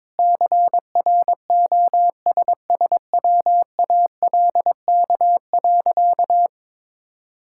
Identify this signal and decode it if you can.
Morse code, heard as a keyed tone, about 21 wpm, 700 Hz. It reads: CROSSWALK.
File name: practice / 22wpm / Easy Sentences with Hints - 22wpm